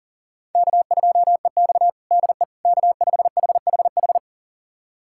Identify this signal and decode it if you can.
Morse code, heard as a keyed tone, about 40 wpm, 700 Hz. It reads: K2EX DE K5HHH
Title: USA Callsign DE Callsign 40wpm